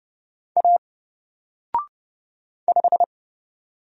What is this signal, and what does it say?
Morse code, heard as a keyed tone, about 30 wpm, 700 Hz. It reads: A 5
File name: Single-Letter-Number-30wpm